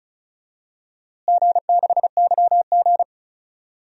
Morse code, heard as a keyed tone, about 36 wpm, 700 Hz. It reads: G6YG
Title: DXCC-100 Call Signs - 4 Characters - Repeated 2x - 35wpm